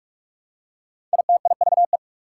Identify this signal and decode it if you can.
Morse code, heard as a keyed tone, about 44 wpm, 690 Hz. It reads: ITIVE
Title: Prefixes and Suffixes 45wpm